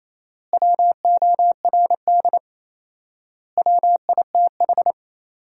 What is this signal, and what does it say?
Morse code, heard as a keyed tone, about 28 wpm, 700 Hz. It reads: WORD WITH
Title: 4x Word-Spacing - Sentences from Top 300 Words - 28wpm